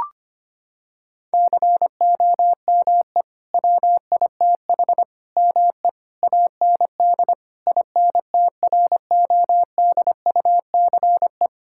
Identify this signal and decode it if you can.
Morse code, heard as a keyed tone, about 25 wpm, 700 Hz. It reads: COME WITH ME AND INTRODUCE
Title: Sentences from Top 1600 Words 25wpm